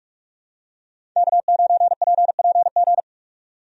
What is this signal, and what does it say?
Morse code, heard as a keyed tone, about 46 wpm, 700 Hz. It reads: K9PPG
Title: USA Call Signs - 5 Characters - 45wpm